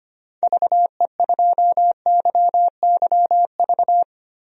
Morse code, heard as a keyed tone, about 25 wpm, 700 Hz. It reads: VE2YYV
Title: DXCC-100 Call Signs - 6 Characters - Repeated 2x - 25wpm